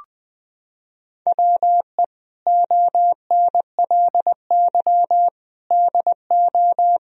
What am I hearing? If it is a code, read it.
Morse code, heard as a keyed tone, about 20 wpm, 700 Hz. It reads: WE ONLY DO
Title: Sentences from Top 1000 Words - Review Part 4 - 20wpm